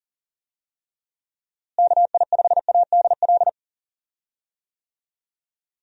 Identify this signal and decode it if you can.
Morse code, heard as a keyed tone, about 38 wpm, 700 Hz. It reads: KI5ADL
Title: USA Call Signs - 6 Characters - Repeated 3x - 40wpm